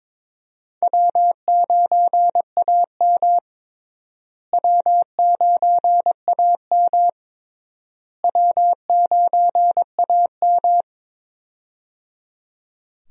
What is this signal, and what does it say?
Morse code, heard as a keyed tone, about 22 wpm, 700 Hz. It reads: W9AM W9AM W9AM
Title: USA Call Signs - 4 Characters - Repeated 3x - 22wpm